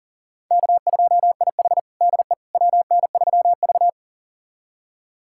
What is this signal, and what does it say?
Morse code, heard as a keyed tone, about 40 wpm, 700 Hz. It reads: K2IH DE WN3V
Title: USA Callsign DE Callsign - 4 Characters - 40wpm